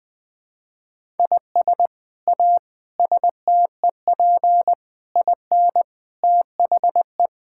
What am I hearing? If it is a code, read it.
Morse code, heard as a keyed tone, about 20 wpm, 700 Hz. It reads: IS A STEP IN THE
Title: Sets of 8 Words - 20wpm